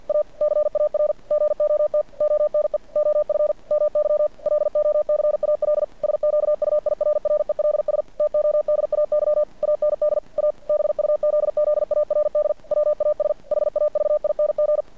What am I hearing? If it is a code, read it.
Morse code, heard as a keyed tone, about 50 wpm, 590 Hz. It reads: A CAR GOT ONE OF MY LOCAL SQUIRRELS TODAY AND A BUZZARD WAS HAVING
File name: archived / CRSnet